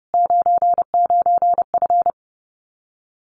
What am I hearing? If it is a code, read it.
Morse code, heard as a keyed tone, about 30 wpm, 700 Hz. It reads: 99F
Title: Common QSO Phrases Encore 30wpm